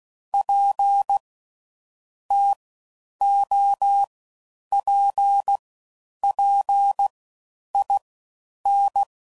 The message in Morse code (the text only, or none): PTOPPIN